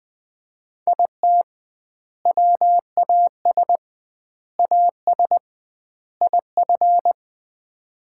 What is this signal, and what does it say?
Morse code, heard as a keyed tone, about 20 wpm, 700 Hz. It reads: IT WAS AS IF
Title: 2x Word-Spacing - Sets of 4 Words - 20wpm